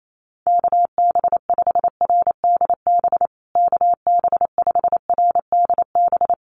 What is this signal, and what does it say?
Morse code, heard as a keyed tone, about 28 wpm, 700 Hz. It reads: KB5RDB KB5RDB
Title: USA Call Signs Repeated 3x 28wpm